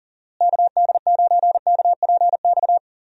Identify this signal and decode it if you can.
Morse code, heard as a keyed tone, about 38 wpm, 700 Hz. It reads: KD9KPX
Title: USA Call Signs - 6 Characters - Repeated 3x - 40wpm